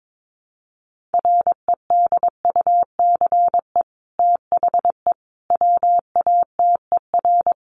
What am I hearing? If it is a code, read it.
Morse code, heard as a keyed tone, about 22 wpm, 700 Hz. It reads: REDUCE THE WATER
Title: Sentences from Top 1100 Words 22wpm